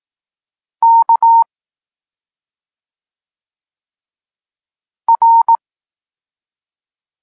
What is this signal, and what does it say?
Morse code, heard as a keyed tone, about 18 wpm, 920 Hz. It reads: K R
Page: Tekst 32 – Doskonalenie odbioru poznanych liter